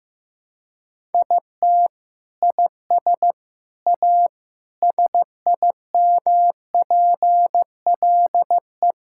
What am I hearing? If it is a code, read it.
Morse code, heard as a keyed tone, about 16 wpm, 700 Hz. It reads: IT IS A SIMPLE
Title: Sentences from Top 1000 Words - Review Part 6 - 15wpm